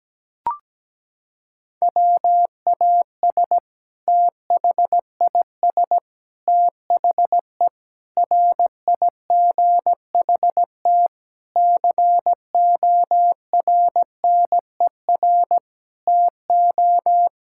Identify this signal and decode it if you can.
Morse code, heard as a keyed tone, about 17 wpm, 700 Hz. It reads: WAS THIS THE RIGHT CORNER TO
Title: Sentences from Top 1600 Words 17wpm